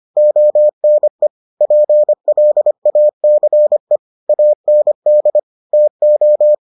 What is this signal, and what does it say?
Morse code, heard as a keyed tone, about 25 wpm, 600 Hz. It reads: ONE PLACE AND TO